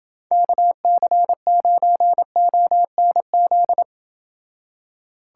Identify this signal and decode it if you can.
Morse code, heard as a keyed tone, about 28 wpm, 700 Hz. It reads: KC9ONZ